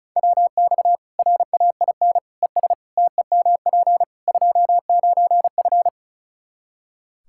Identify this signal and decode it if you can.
Morse code, heard as a keyed tone, about 34 wpm, 700 Hz. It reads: WX RAIN ES TEMP 29F
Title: Practice QSO - Exchange 2 - Info - 35wpm